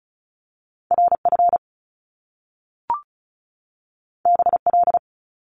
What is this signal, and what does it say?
Morse code, heard as a keyed tone, about 36 wpm, 700 Hz. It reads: RF BL
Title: Top Two Letter Combinations in Words 35wpm